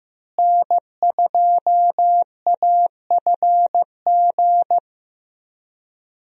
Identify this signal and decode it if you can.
Morse code, heard as a keyed tone, about 15 wpm, 700 Hz. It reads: N2AFG
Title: USA Call Signs - 5 Characters - Repeated 3x - 15wpm